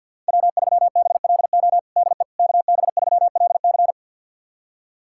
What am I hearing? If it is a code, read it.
Morse code, heard as a keyed tone, about 51 wpm, 700 Hz. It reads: W3BLO DE KB3LC
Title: USA Callsign DE Callsign - 5 Characters - 50wpm